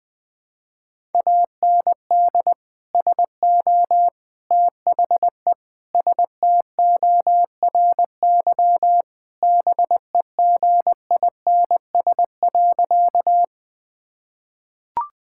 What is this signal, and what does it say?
Morse code, heard as a keyed tone, about 20 wpm, 700 Hz. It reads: AND SO THE STORY BEGINS.
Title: Sentences with 2-Word Phrases - Spoken Beginning - 20wpm